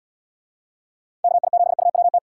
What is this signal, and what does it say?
Morse code, heard as a keyed tone, about 78 wpm, 700 Hz. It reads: KI7UWT